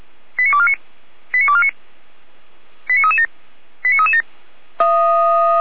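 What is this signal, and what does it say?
Five-tone selective call (ZVEI), heard as quick runs of five tones, tones of about 70 ms. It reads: 89249 89249 89337 89337
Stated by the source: ZVEI five-tone call received by SDR.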